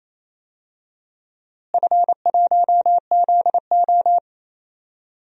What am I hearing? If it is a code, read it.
Morse code, heard as a keyed tone, about 28 wpm, 700 Hz. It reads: F1ZO
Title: DXCC-100 Call Signs - 4 Characters - Repeated 2x - 28wpm